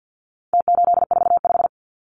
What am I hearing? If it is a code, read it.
Morse code, heard as a keyed tone, about 50 wpm, 700 Hz. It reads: T845
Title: Numbers - Contest Style - 4-Digit 50wpm